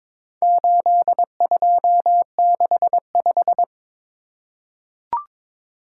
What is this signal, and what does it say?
Morse code, heard as a keyed tone, about 22 wpm, 700 Hz. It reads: 8265
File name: practice / 22wpm / Four Letter-Number Combinations - 22wpm